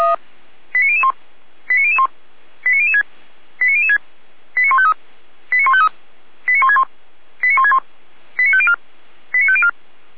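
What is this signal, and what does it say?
Five-tone selective call (ZVEI), heard as quick runs of five tones, tones of about 70 ms. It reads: 89001 89001 89006 89006 89153 89153 89161 89161 89504 89504
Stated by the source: ZVEI five-tone call 89099 received on SDR.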